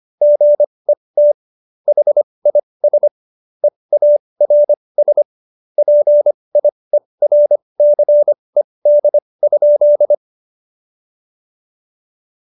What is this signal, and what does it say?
Morse code, heard as a keyed tone, about 25 wpm, spaced as for 20 wpm, 590 Hz. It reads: GET HIS EARS PIERCED?
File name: Jokes-25-char-20-eff